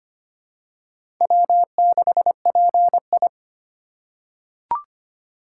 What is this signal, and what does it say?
Morse code, heard as a keyed tone, about 25 wpm, 700 Hz. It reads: W6PI
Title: USA Call Signs - 4 Characters - 2x Character Spacing - 25wpm